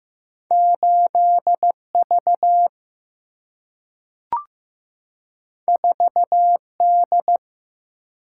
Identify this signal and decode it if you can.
Morse code, heard as a keyed tone, about 15 wpm, 700 Hz. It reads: 8V 4D
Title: Single-Letter Single-Number 15wpm